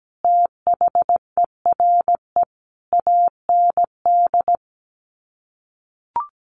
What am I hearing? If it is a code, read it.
Morse code, heard as a keyed tone, about 17 wpm, 700 Hz. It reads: THERE AND
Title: Sets of 5 Words Encore2 17wpm